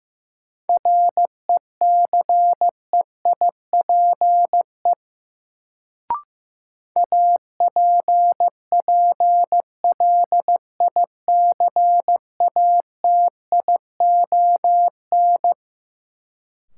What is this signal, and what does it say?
Morse code, heard as a keyed tone, about 15 wpm, 700 Hz. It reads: RECEIPE APPLICATION
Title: Sentences from Top 1200 Words - New Word Drill - 15wpm